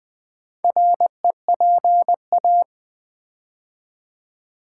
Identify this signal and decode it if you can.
Morse code, heard as a keyed tone, about 20 wpm, 700 Hz. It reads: REPA
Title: Build A Word - 2 Letters At A Time - 6-Letter Words - 20wpm